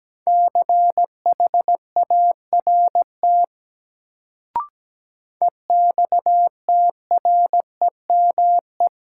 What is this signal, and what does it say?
Morse code, heard as a keyed tone, about 17 wpm, 700 Hz. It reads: CHART EXTREME
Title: Sentences from Top 1300 Words - New Word Drill - 17wpm